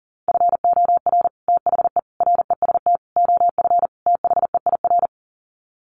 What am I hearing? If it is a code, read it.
Morse code, heard as a keyed tone, about 37 wpm, 700 Hz. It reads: FOR THE REST OF THEIR
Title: Sets of 5 Words Encore2 40wpm